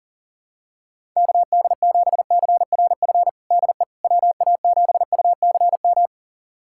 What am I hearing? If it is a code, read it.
Morse code, heard as a keyed tone, about 40 wpm, 700 Hz. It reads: KD7CRF DE WA7UCM
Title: USA Callsign DE Callsign - 6 Characters - 40wpm